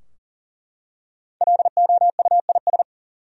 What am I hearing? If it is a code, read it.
Morse code, heard as a keyed tone, about 40 wpm, 700 Hz. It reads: LOUIS
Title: USA Common Names - 40wpm